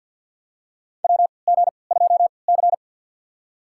Morse code, heard as a keyed tone, about 48 wpm, spaced as for 28 wpm, 700 Hz. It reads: WG2C
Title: USA Call Signs - 4 Characters - 3x Character Spacing - 50wpm